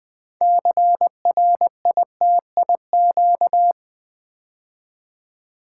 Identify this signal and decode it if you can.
Morse code, heard as a keyed tone, about 20 wpm, 700 Hz. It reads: CRITIQ
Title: Build A Word - 3 Letters At A Time - 8-Letter Words - 20wpm